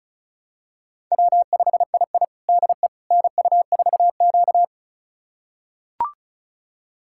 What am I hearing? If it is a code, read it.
Morse code, heard as a keyed tone, about 34 wpm, 700 Hz. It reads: W5II DE NU4Q
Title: USA Callsign DE Callsign - 4 Characters - 35wpm